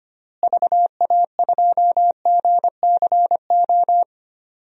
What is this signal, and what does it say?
Morse code, heard as a keyed tone, about 25 wpm, 700 Hz. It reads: VA2GCO
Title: DXCC-100 Call Signs - 6 Characters - Repeated 2x - 25wpm